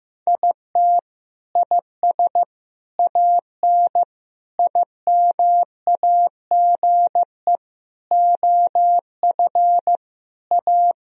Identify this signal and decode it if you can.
Morse code, heard as a keyed tone, about 15 wpm, 700 Hz. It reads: IT IS AN IMAGE OF A
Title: Sentences from Top 1000 Words - Review Part 4 - 15wpm